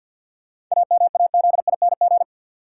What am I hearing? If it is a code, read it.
Morse code, heard as a keyed tone, about 52 wpm, 700 Hz. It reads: AMAZING
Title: Sentences from Top 1500 Words - New Word Drill - 50wpm